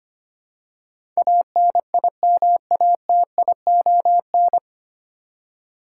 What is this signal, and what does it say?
Morse code, heard as a keyed tone, about 25 wpm, 690 Hz. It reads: ANIMATION
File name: practice / 25wpm / 9 Letter Words - First 3 Letters Given - 25wpm